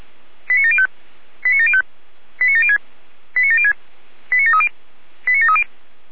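Five-tone selective call (ZVEI), heard as quick runs of five tones, tones of about 70 ms. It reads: 89705 89705 89706 89706 89730 89730
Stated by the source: SDR-received ZVEI five-tone selective call, code 89750.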